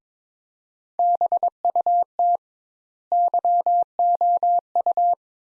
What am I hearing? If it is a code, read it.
Morse code, heard as a keyed tone, about 22 wpm, 700 Hz. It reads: BUT YOU
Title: Sentences Top 200 Words - 22wpm